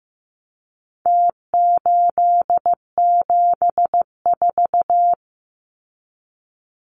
Morse code, heard as a keyed tone, about 15 wpm, 700 Hz. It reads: T874
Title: Numbers - Contest Style - 4-Digit 15wpm